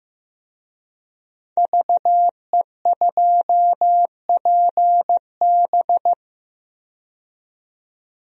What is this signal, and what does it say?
Morse code, heard as a keyed tone, about 15 wpm, 700 Hz. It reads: VE2PB
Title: DXCC-100 Call Signs - 5 Characters - 15wpm